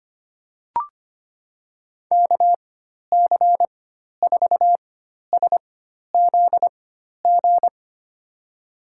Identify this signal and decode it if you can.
Morse code, heard as a keyed tone, about 25 wpm, spaced as for 12 wpm, 700 Hz. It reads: KC4SZG
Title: USA Call Signs - 6 Characters - 4x Character Spacing - 25wpm